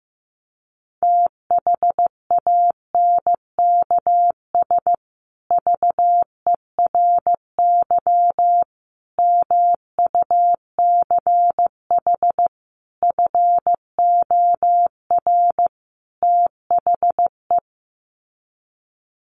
Morse code, heard as a keyed tone, about 15 wpm, 700 Hz. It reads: THANKS VERY MUCH FOR THE
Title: Sets of 5 Words Encore2 15wpm